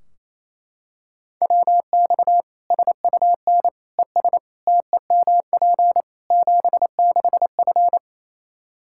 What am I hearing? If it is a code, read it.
Morse code, heard as a keyed tone, about 27 wpm, 700 Hz. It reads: WX SUN ES TEMP 76F
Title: Practice QSO - Exchange 2 - Info - 28wpm